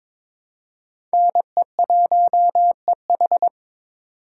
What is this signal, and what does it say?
Morse code, heard as a keyed tone, about 22 wpm, 700 Hz. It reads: NE1EH
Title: USA Call Signs - 5 Characters - 2x Character Spacing - 22wpm